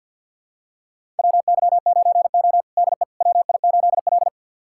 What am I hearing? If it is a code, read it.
Morse code, heard as a keyed tone, about 53 wpm, 700 Hz. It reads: WY9O DE WI8L
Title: USA Callsign DE Callsign - 4 Characters - 50wpm